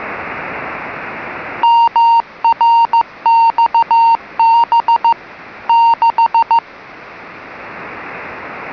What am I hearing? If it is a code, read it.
Morse code, heard as a keyed tone, about 15 wpm, 920 Hz. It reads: MRXB 6